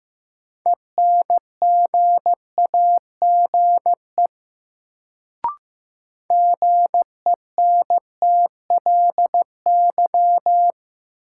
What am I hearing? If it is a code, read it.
Morse code, heard as a keyed tone, about 15 wpm, 700 Hz. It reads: ENGAGE GENTLY
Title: Sentences from Top 1500 Words - New Word Drill Alt - 15wpm